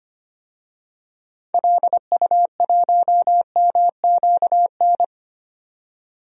Morse code, heard as a keyed tone, about 25 wpm, 700 Hz. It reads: LU1MQN
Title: DXCC-100 Call Signs - 6 Characters - 25wpm